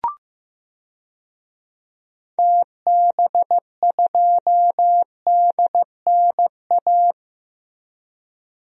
Morse code, heard as a keyed tone, about 15 wpm, 700 Hz. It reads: TB2DNA
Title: DXCC-100 Call Signs - 6 Characters - Repeated 2x - 15wpm